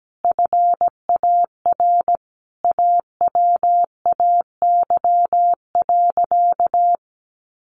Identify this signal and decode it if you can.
Morse code, heard as a keyed tone, about 17 wpm, 700 Hz. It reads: FAR AWAY.
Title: Sentences from Top 1500 Words 17wpm